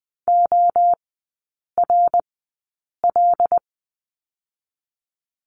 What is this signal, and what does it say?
Morse code, heard as a keyed tone, about 19 wpm, spaced as for 8 wpm, 700 Hz. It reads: ORL
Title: 191 Lesson 31e - Letter v - Three Characters 20wpm